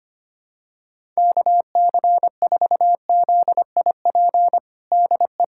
Morse code, heard as a keyed tone, about 25 wpm, 700 Hz. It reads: KC4ZIP DE
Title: USA Callsign DE Callsign 25wpm